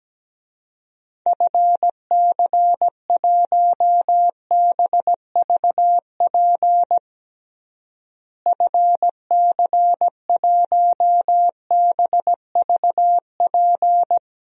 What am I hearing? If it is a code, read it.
Morse code, heard as a keyed tone, about 17 wpm, 700 Hz. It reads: FC1BVP FC1BVP
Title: DXCC-100 Call Signs - 6 Characters - Repeated 3x - 17wpm